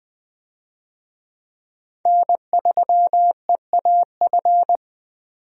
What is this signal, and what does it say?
Morse code, heard as a keyed tone, about 20 wpm, 700 Hz. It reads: N3EAF
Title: USA Call Signs - 5 Characters - 20wpm